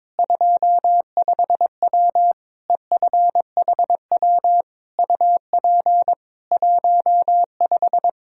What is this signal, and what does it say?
Morse code, heard as a keyed tone, about 22 wpm, 700 Hz. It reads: 25W EFHW UP 15
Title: Practice QSO - Exchange 2 - Info - 22wpm